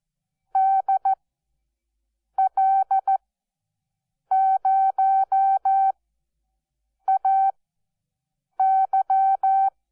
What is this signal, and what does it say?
Morse code, heard as a keyed tone, about 14 wpm, spaced as for 6 wpm, 770 Hz. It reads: DL0AY
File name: Onlinelehrgang / morsen / DJ4UF Morsen12